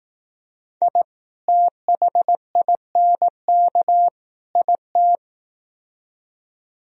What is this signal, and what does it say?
Morse code, heard as a keyed tone, about 18 wpm, 700 Hz. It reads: I THINK IT
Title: Speed-Racing - Sets of 3 Words 15wpm